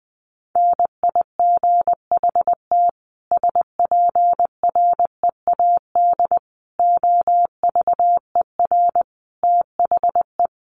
Morse code, heard as a keyed tone, about 20 wpm, 700 Hz. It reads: NIGHT SPREAD OVER THE
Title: Sentences from Top 1200 Words 20wpm